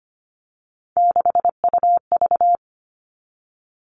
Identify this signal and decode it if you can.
Morse code, heard as a keyed tone, about 25 wpm, 700 Hz. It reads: 6UV
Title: All Cut Numbers - 3-Digit 25wpm